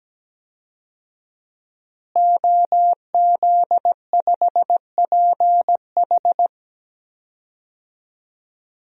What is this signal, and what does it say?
Morse code, heard as a keyed tone, about 17 wpm, 700 Hz. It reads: OZ5PH
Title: DXCC-100 Call Signs - 5 Characters - Repeated 3x - 17wpm